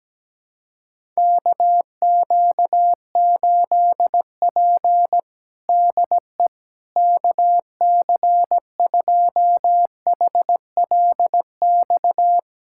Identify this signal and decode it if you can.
Morse code, heard as a keyed tone, about 17 wpm, 700 Hz. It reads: KQ8P DE KC2HLX
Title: USA Callsign DE Callsign 17wpm